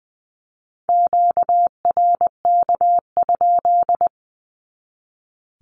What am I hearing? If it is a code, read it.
Morse code, heard as a keyed tone, about 20 wpm, 700 Hz. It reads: QRK?
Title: Q-Codes 20wpm